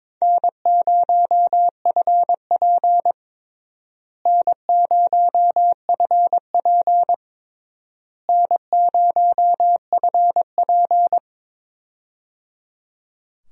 Morse code, heard as a keyed tone, about 22 wpm, 700 Hz. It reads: N0FP N0FP N0FP
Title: USA Call Signs - 4 Characters - Repeated 3x - 22wpm